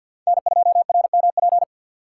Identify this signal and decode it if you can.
Morse code, heard as a keyed tone, about 53 wpm, 700 Hz. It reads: N2RMP
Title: USA Call Signs - 5 Characters - 3x Character Spacing - 50wpm